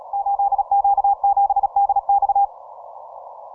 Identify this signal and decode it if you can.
Morse code, heard as a keyed tone, about 36 wpm, 800 Hz. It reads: 8Q7DX
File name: simple-file-list / 8Q7DX 15MCW